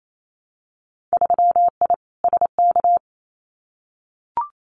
Morse code, heard as a keyed tone, about 28 wpm, 700 Hz. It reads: 3I SK
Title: Winter Field Day Exchanges Encore 28wpm